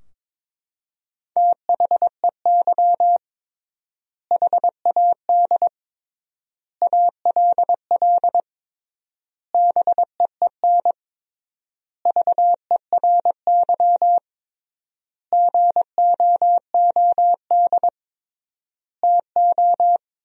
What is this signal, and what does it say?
Morse code, heard as a keyed tone, about 22 wpm, 700 Hz. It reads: THEY HAD ALL BEEN VERY GOOD TO
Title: 3x Word-Spacing - Sentences from Top 200 Words - 22wpm